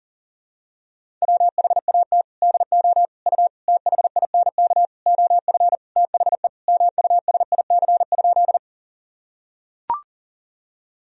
Morse code, heard as a keyed tone, about 40 wpm, 700 Hz. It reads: WHAT DO U THINK OF THE MUSIC?
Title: Sentences from Top 300 Words and Abbreviations 40wpm